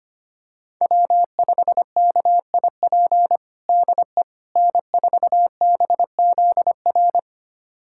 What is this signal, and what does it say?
Morse code, heard as a keyed tone, about 25 wpm, 700 Hz. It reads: W5KIP DE N4BZR
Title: USA Callsign DE Callsign - 5 Characters - 25wpm